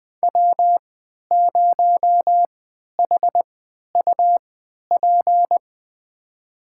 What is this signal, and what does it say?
Morse code, heard as a keyed tone, about 21 wpm, spaced as for 11 wpm, 700 Hz. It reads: W0HUP
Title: USA Call Signs - 5 Characters - 3x Character Spacing - 20wpm